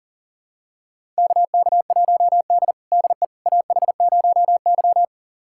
Morse code, heard as a keyed tone, about 37 wpm, 700 Hz. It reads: KK1D DE AH0Y
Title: USA Callsign DE Callsign - 4 Characters - 40wpm